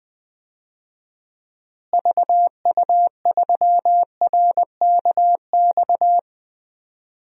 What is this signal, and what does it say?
Morse code, heard as a keyed tone, about 20 wpm, 700 Hz. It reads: VU3RKX
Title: DXCC-100 Call Signs - 6 Characters - Repeated 3x - 20wpm